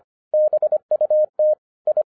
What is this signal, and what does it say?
Morse code, heard as a keyed tone, about 24 wpm, 610 Hz. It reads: BUT I
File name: FIRST FISTS CONTACT-25wpm